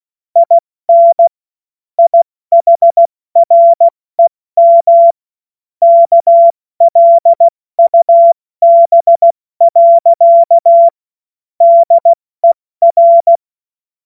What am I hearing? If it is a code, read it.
Morse code, heard as a keyed tone, about 16 wpm, spaced as for 14 wpm, 680 Hz. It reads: IN IHREM KLUB. DER
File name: Morsekurs / Witze / Witz 15. Februar 2022 70 F80